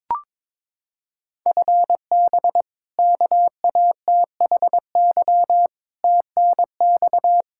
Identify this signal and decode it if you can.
Morse code, heard as a keyed tone, about 22 wpm, 700 Hz. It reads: FB KATHY TNX
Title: Practice QSO - Exchange 3 - More Info - 22wpm